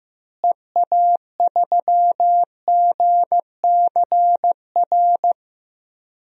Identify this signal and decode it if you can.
Morse code, heard as a keyed tone, about 14 wpm, 700 Hz. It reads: EA3GCR
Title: DXCC-100 Call Signs - 6 Characters - Repeated 3x - 15wpm